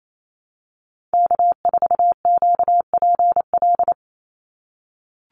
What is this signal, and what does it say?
Morse code, heard as a keyed tone, about 28 wpm, 700 Hz. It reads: K4QPL
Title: Call Signs - 28wpm